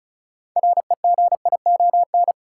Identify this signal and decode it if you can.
Morse code, heard as a keyed tone, about 36 wpm, 700 Hz. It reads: REGION
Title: Sentences from Top 1500 Words - New Word Drill - 35wpm